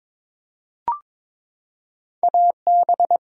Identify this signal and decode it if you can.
Morse code, heard as a keyed tone, about 22 wpm, 700 Hz. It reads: AB
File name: ARRL Sections v2 - 22wpm